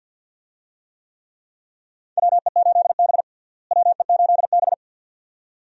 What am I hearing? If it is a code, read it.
Morse code, heard as a keyed tone, about 52 wpm, 700 Hz. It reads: WE8B WE8B
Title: USA Call Signs - 4 Characters - Repeated 2x - 50wpm